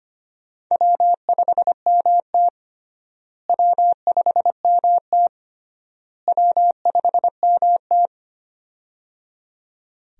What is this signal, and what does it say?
Morse code, heard as a keyed tone, about 25 wpm, 700 Hz. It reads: W5MT W5MT W5MT
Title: USA Call Signs - 4 Characters - Repeated 3x - 25wpm